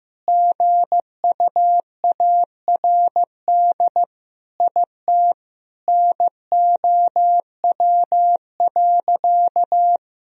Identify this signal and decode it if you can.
Morse code, heard as a keyed tone, about 15 wpm, 700 Hz. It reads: GUARD IT NOW.
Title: Sentences from Top 1000 Words - Review Part 4 - 15wpm